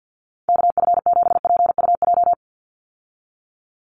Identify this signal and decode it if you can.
Morse code, heard as a keyed tone, about 50 wpm, 700 Hz. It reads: KF7PUJ
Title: USA Call Signs 50wpm